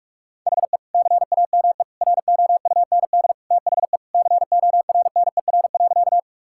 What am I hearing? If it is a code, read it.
Morse code, heard as a keyed tone, about 46 wpm, 700 Hz. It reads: HE CAME ROUND THE CORNER.
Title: Sentences from Top 1600 Words 45wpm